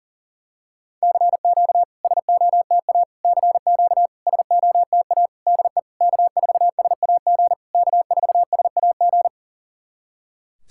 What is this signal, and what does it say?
Morse code, heard as a keyed tone, about 40 wpm, 700 Hz. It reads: CQ SOTA CQ SOTA DE K4SAG K4SAG
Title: SOTA QSO Practice 40wpm